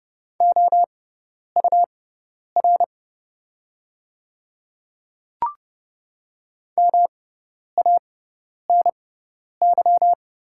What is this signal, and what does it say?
Morse code, heard as a keyed tone, about 30 wpm, spaced as for 10 wpm, 700 Hz. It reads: OUR MANY
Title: 6x Character Spacing - Top 200 Words - 30wpm